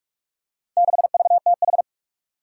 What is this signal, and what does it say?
Morse code, heard as a keyed tone, about 44 wpm, 700 Hz. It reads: 6VTH
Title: Four Challenging Letter-Number Combinations 45wpm